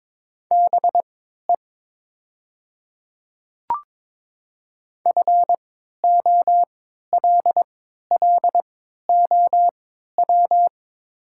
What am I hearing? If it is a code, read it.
Morse code, heard as a keyed tone, about 22 wpm, spaced as for 12 wpm, 690 Hz. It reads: BE FOLLOW
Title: 3x Character Spacing - Top 200 Words - 22wpm